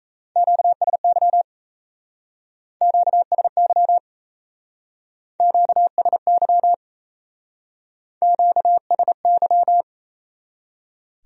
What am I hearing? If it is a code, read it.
Morse code, heard as a keyed tone, about 33 wpm, 700 Hz. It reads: QSY QSY QSY QSY
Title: Speed-Racing Q-Codes - 28wpm